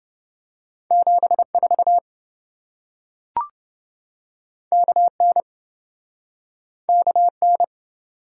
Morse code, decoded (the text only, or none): 74 KN KN